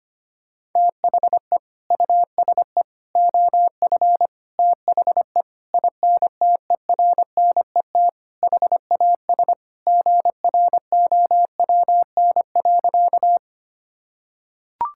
Morse code, heard as a keyed tone, about 25 wpm, 700 Hz. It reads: THE USE OF THE INTERNET HAS GROWN.
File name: practice / 25wpm / Sentences with 4-Word Phrases - 25wpm